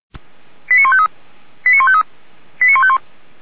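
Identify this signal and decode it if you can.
Five-tone selective call (ZVEI), heard as quick runs of five tones, tones of about 70 ms. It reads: 89153 89153 89151
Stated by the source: ZVEI five-tone fire brigade alert received by SDR, Feuerwehr Orlamünde.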